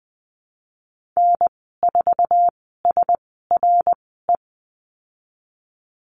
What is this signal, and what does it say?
Morse code, heard as a keyed tone, about 20 wpm, spaced as for 15 wpm, 700 Hz. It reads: N4SRE
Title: 088 Lesson 15g - Number 5 - Call Signs 20wpm